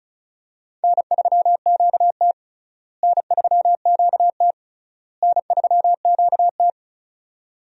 Morse code, heard as a keyed tone, about 36 wpm, 700 Hz. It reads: N3QT N3QT N3QT
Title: USA Call Signs - 4 Characters - Repeated 3x - 35wpm